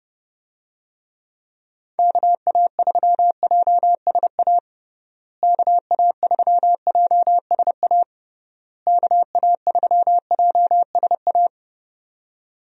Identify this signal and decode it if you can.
Morse code, heard as a keyed tone, about 32 wpm, 700 Hz. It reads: KA3JSA KA3JSA KA3JSA
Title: USA Call Signs - 6 Characters - Repeated 3x - 30wpm